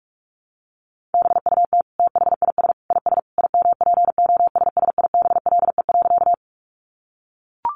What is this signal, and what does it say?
Morse code, heard as a keyed tone, about 44 wpm, 700 Hz. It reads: BUT THIS IS IMPOSSIBLE.
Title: Sentences from Top 1100 Words 45wpm